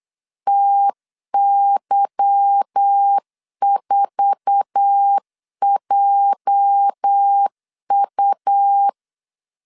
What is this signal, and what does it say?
Morse code, heard as a keyed tone, about 8 wpm, 800 Hz. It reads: TY4JU
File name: LEZIONI / 8 YJ45 / 8 YJ4 8wpm 3246